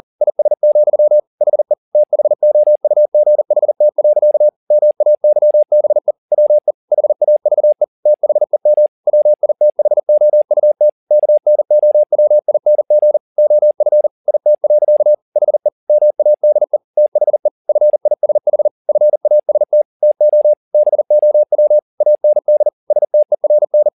Morse code, heard as a keyed tone, about 40 wpm, 600 Hz. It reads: IS, HE THOUGHT. MAYBE WE HAVE THEM WITHOUT KNOWING OF IT. HE MADE THE FISH FAST TO BOW AND STERN
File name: the old man and the sea 2 40wpm